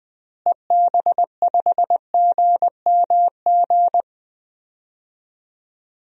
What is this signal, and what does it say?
Morse code, heard as a keyed tone, about 20 wpm, 700 Hz. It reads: EB5GMG
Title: DXCC-100 Call Signs - 6 Characters - Repeated 3x - 20wpm